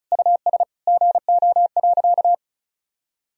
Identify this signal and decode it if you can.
Morse code, heard as a keyed tone, about 34 wpm, 700 Hz. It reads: US GO.
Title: Sentences from Top 1000 Words - Review Part 2 - 35wpm